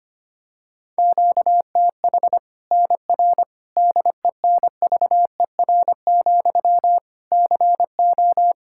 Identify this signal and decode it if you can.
Morse code, heard as a keyed tone, about 25 wpm, 700 Hz. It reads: QTH NR DENVER, CO
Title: K1USN SST QSO Practice 25wpm